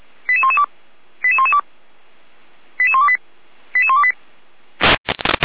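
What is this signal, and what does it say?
Five-tone selective call (ZVEI), heard as quick runs of five tones, tones of about 70 ms. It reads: 88112 88112 88128 88128
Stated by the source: ZVEI five-tone selective call received on an SDR.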